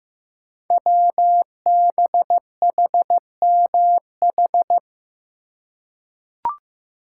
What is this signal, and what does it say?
Morse code, heard as a keyed tone, about 15 wpm, 700 Hz. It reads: WBHMH
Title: Code Groups - 5 Letter:Numbers - 15wpm